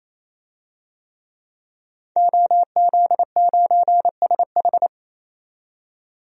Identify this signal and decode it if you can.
Morse code, heard as a keyed tone, about 28 wpm, 700 Hz. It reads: OZ9SH
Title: DXCC-100 Call Signs - 5 Characters - 28wpm